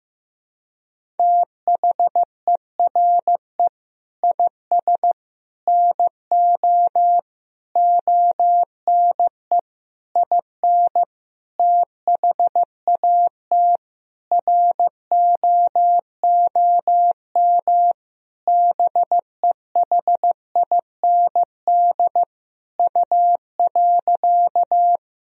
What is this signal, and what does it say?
Morse code, heard as a keyed tone, about 15 wpm, 700 Hz. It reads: THERE IS NO ONE IN THAT ROOM BEHIND U.
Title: Sentences from Top 500 Words and Abbreviations 15wpm